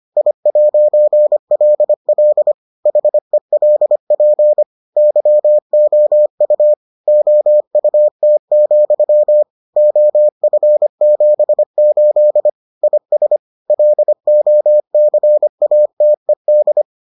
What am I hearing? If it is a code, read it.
Morse code, heard as a keyed tone, about 25 wpm, 600 Hz. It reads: I'LL HELP YOU OUT, OF78 IS LOCATED